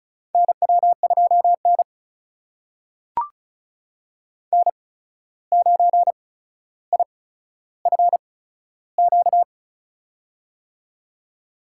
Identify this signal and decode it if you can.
Morse code, heard as a keyed tone, about 36 wpm, spaced as for 14 wpm, 700 Hz. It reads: NW2N N9IFQ
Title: USA Call Signs - 8x Character Spacing - 35wpm